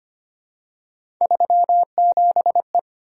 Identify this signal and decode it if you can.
Morse code, heard as a keyed tone, about 25 wpm, 700 Hz. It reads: 37E
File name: Three Letter-Number Combinations - 25wpm